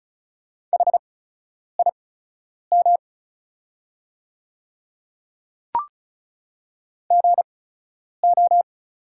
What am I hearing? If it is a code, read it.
Morse code, heard as a keyed tone, about 36 wpm, spaced as for 10 wpm, 700 Hz. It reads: HIM GO
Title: 8x Character Spacing - Top 100 Words - 35wpm